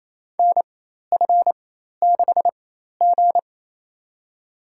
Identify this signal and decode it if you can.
Morse code, heard as a keyed tone, about 28 wpm, spaced as for 13 wpm, 700 Hz. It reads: NF6G
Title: USA Call Signs - 4 Characters - 4x Character Spacing - 28wpm